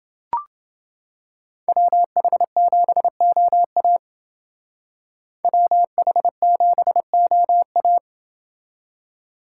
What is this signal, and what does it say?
Morse code, heard as a keyed tone, about 29 wpm, 700 Hz. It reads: WH7OA WH7OA